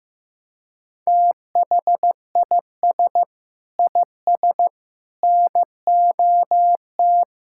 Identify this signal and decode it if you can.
Morse code, heard as a keyed tone, about 15 wpm, 700 Hz. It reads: THIS IS NOT
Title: Sentences from Top 1000 Words - Review Part 6 - 15wpm